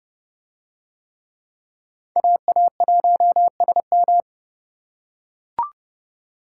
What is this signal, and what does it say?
Morse code, heard as a keyed tone, about 30 wpm, 700 Hz. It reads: AA1SM